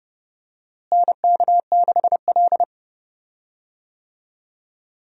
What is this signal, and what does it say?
Morse code, heard as a keyed tone, about 30 wpm, 700 Hz. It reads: NK6L
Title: USA Call Signs - 4 Characters - Repeated 2x - 30wpm